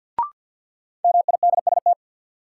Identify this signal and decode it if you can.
Morse code, heard as a keyed tone, about 52 wpm, 700 Hz. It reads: MIDST
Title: Sets of 4 Words - New Word Drill - 50wpm